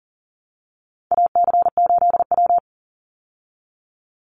Morse code, heard as a keyed tone, about 39 wpm, 700 Hz. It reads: AC8W
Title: Call Signs Repeated 3x - 40wpm